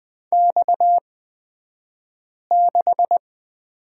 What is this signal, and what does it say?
Morse code, heard as a keyed tone, about 20 wpm, spaced as for 5 wpm, 700 Hz. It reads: X6
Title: 228 Lesson 37d - Letter x - Two Characters 20wpm